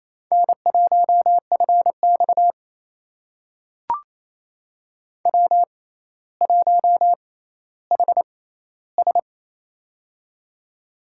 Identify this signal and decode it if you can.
Morse code, heard as a keyed tone, about 28 wpm, spaced as for 14 wpm, 700 Hz. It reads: N1FX W1HS
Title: USA Call Signs - 4 Characters - 6x Character Spacing - 28wpm